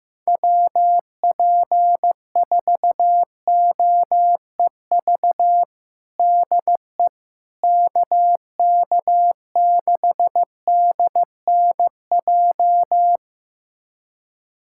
Morse code, heard as a keyed tone, about 15 wpm, 700 Hz. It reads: WP4OEV DE KK6DNJ
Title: USA Callsign DE Callsign - 6 Characters - 15wpm